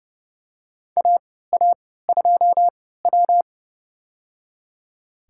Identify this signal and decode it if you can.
Morse code, heard as a keyed tone, about 30 wpm, spaced as for 17 wpm, 700 Hz. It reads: AA2W
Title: USA Call Signs - 3x Character Spacing - 30wpm